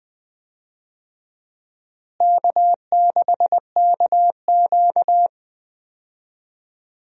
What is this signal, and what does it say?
Morse code, heard as a keyed tone, about 21 wpm, 700 Hz. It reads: K6KQ
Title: USA Call Signs - 4 Characters - Repeated 2x - 20wpm